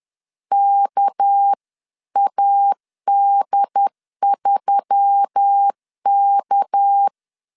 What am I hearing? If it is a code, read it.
Morse code, heard as a keyed tone, about 10 wpm, 800 Hz. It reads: KAD3K
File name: LEZIONI / 6 PKX0 / 6 PK 10wpm 2436